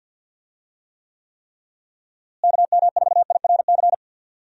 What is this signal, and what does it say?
Morse code, heard as a keyed tone, about 46 wpm, 700 Hz. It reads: KM4IRC
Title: USA Call Signs - 6 Characters - Repeated 3x - 50wpm